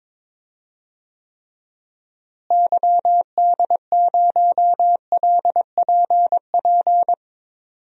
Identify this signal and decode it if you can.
Morse code, heard as a keyed tone, about 21 wpm, 700 Hz. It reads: YD0LPP
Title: DXCC-100 Call Signs - 6 Characters - Repeated 2x - 22wpm